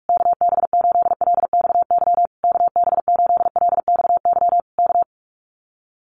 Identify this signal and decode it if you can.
Morse code, heard as a keyed tone, about 46 wpm, 700 Hz. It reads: KB8LXY KB8LXY K
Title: Common QSO Phrases Encore 45wpm